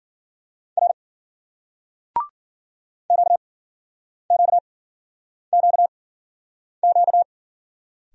Morse code, heard as a keyed tone, about 51 wpm, spaced as for 9 wpm, 700 Hz. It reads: R QQQQ